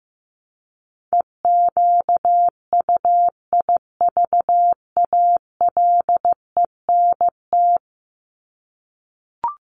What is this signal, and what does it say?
Morse code, heard as a keyed tone, about 15 wpm, 700 Hz. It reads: EQUIVALENT
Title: Top 1501-1750 Words 15wpm